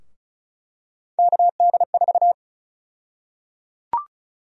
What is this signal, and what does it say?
Morse code, heard as a keyed tone, about 36 wpm, 700 Hz. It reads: KD4
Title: USA Call Sign Prefixes - Repeated 2x - 35wpm